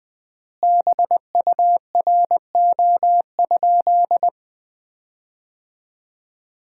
Morse code, heard as a keyed tone, about 20 wpm, 700 Hz. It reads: BURO?
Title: Common QSO Phrases - 3x Word-Spacing - 20wpm